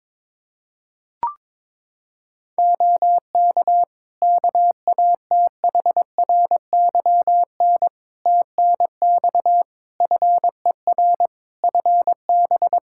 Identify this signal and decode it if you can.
Morse code, heard as a keyed tone, about 22 wpm, 700 Hz. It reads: OK KATHRYN TNX FER FB
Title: Practice QSO - Exchange 4 - Ending - 22wpm